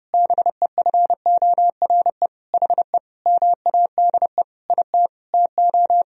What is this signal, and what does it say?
Morse code, heard as a keyed tone, about 30 wpm, 700 Hz. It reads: BEFORE HE MADE IT TO
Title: Sentences from Top 1000 Words - Review Part 2 - 30wpm